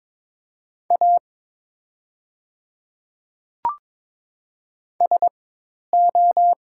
Morse code, heard as a keyed tone, about 22 wpm, spaced as for 10 wpm, 700 Hz. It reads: A SO
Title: 4x Character Spacing - Top 200 Words - 22wpm